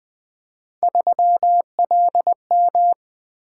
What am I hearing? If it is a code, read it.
Morse code, heard as a keyed tone, about 20 wpm, 700 Hz. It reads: 3LM
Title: Three Letter-Number Combinations - 20wpm